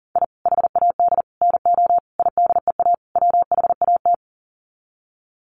Morse code, heard as a keyed tone, about 40 wpm, 700 Hz. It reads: I HAD NO IDEA WHAT
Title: Sets of 5 Words Encore 40wpm